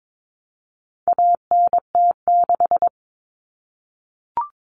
Morse code, heard as a keyed tone, about 22 wpm, 700 Hz. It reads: ANT6
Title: Common Cut Numbers - 4-Digit 22wpm